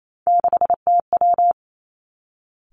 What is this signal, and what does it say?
Morse code, heard as a keyed tone, about 28 wpm, 700 Hz. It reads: 6TW
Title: Three Challenging Letter-Number Combinations 28wpm